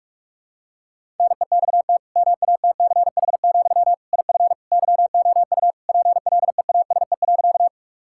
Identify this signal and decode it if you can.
Morse code, heard as a keyed tone, about 46 wpm, 700 Hz. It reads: NEXT MATCH, IF YOU PLEASE.
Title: Sentences from Top 1600 Words 45wpm